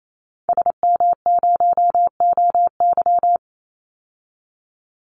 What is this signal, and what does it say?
Morse code, heard as a keyed tone, about 28 wpm, 690 Hz. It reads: SM0OY
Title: Call Signs Repeated 3x - 28wpm